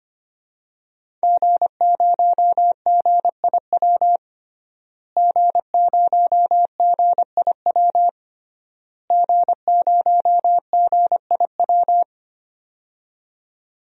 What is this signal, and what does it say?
Morse code, heard as a keyed tone, about 25 wpm, 700 Hz. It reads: G0GIW G0GIW G0GIW
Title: DXCC-100 Call Signs - 5 Characters - Repeated 3x - 25wpm